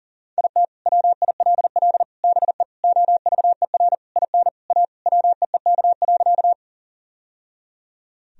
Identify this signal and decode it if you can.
Morse code, heard as a keyed tone, about 40 wpm, 700 Hz. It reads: IT WILL BE OVER IN A WEEK.
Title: Sentences from Top 1000 Words - Review Part 2 - 40wpm